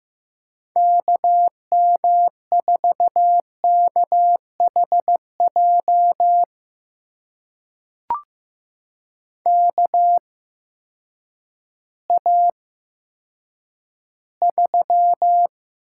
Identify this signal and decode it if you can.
Morse code, heard as a keyed tone, about 15 wpm, spaced as for 9 wpm, 700 Hz. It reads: KM4KHJ KA3
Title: USA Call Signs - 6 Characters - 8x Character Spacing - 15wpm